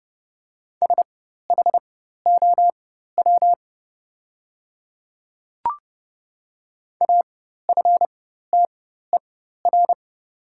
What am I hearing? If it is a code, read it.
Morse code, heard as a keyed tone, about 29 wpm, spaced as for 14 wpm, 700 Hz. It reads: SHOW AFTER
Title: 4x Character Spacing - Top 200 Words - 30wpm